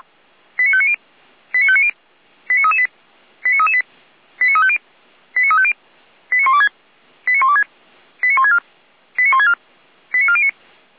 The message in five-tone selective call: 89590 89590 89338 89338 89350 89350 89126 89126 89164 89164 89409

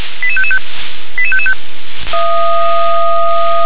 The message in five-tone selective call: 88555 88555

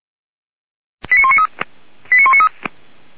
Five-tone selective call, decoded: 89193 89193

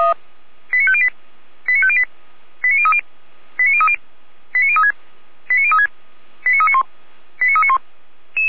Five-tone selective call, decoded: 89508 89508 89030 89030 89036 89036 89391 89391

